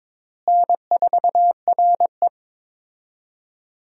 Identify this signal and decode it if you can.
Morse code, heard as a keyed tone, about 22 wpm, 700 Hz. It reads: N4RE
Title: USA Call Signs - 4 Characters - Repeated 3x - 22wpm